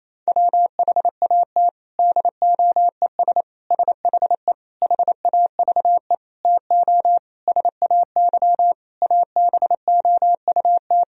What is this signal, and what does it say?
Morse code, heard as a keyed tone, about 28 wpm, 700 Hz. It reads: WHAT DOES SHE HAVE TO SAY ABOUT
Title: Sentences from Top 1000 Words - Review Part 3 - 28wpm